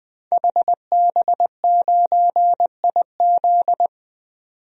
Morse code, heard as a keyed tone, about 21 wpm, 700 Hz. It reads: HB9IZ
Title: DXCC-100 Call Signs - 5 Characters - 20wpm